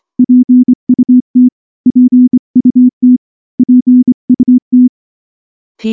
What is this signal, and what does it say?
Morse code, heard as a keyed tone, about 24 wpm, 250 Hz. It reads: PUT PUT PUT
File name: morse / morse-training-25WPM